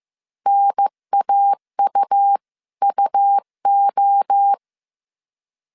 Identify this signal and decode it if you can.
Morse code, heard as a keyed tone, about 15 wpm, 800 Hz. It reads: NAUUO
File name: CW / DET / 1 3 1500C S C 14wpm 2458